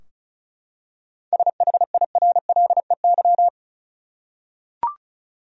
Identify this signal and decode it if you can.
Morse code, heard as a keyed tone, about 36 wpm, 700 Hz. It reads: SHIRLEY
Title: USA Common Names - 35wpm